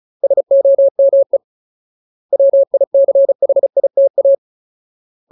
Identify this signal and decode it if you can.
Morse code, heard as a keyed tone, about 34 wpm, 550 Hz. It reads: SOME WICHITA
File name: Ultimate Word List / 35WPM Ultimate